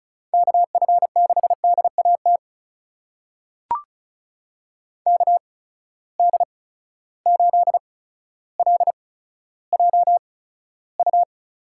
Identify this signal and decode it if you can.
Morse code, heard as a keyed tone, about 36 wpm, spaced as for 15 wpm, 700 Hz. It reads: KF6DAT KD8LJU
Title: USA Call Signs - 6 Characters - 8x Character Spacing - 35wpm